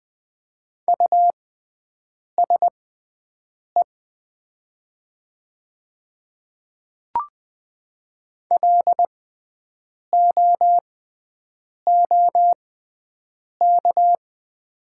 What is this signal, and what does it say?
Morse code, heard as a keyed tone, about 20 wpm, spaced as for 7 wpm, 700 Hz. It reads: USE LOOK
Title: 6x Character Spacing - Top 100 Words - 20wpm